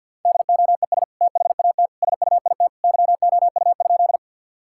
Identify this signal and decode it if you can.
Morse code, heard as a keyed tone, about 52 wpm, 700 Hz. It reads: DOES THAT SUIT YOU?
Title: Sentences from Top 1600 Words 50wpm